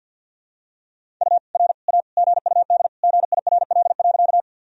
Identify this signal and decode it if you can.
Morse code, heard as a keyed tone, about 51 wpm, 700 Hz. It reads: U R A GUD GIRL.
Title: Sentences from Top 400 Words and Abbreviations 50wpm